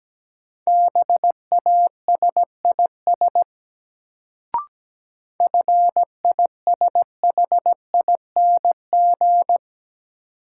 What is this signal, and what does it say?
Morse code, heard as a keyed tone, about 17 wpm, 700 Hz. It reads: BASIS FISHING
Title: Sentences from Top 1000 Words - Review Part 5 - New Word Drill - 17wpm